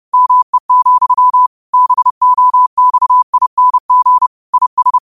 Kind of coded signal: Morse code